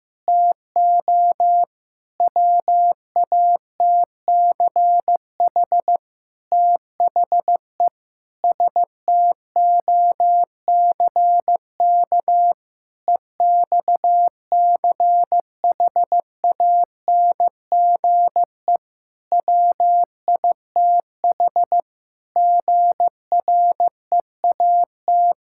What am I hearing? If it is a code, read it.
Morse code, heard as a keyed tone, about 15 wpm, 700 Hz. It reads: TO WATCH THE STOCK EXCHANGE WITH GREAT
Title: Sentences from Top 1000 Words - Review Part 6 - 15wpm